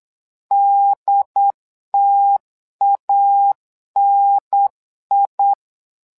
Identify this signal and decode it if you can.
Morse code, heard as a keyed tone, about 8 wpm, 800 Hz. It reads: DTANI
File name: LEZIONI / 2 UHDN / 2 UHDN 8wpm2232